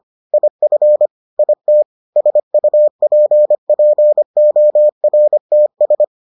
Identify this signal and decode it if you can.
Morse code, heard as a keyed tone, about 25 wpm, 600 Hz. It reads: IF IT SUPPORTS